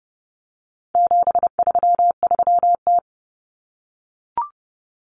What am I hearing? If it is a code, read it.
Morse code, heard as a keyed tone, about 30 wpm, 700 Hz. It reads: 733T
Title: Common Cut Numbers - 4-Digit 30wpm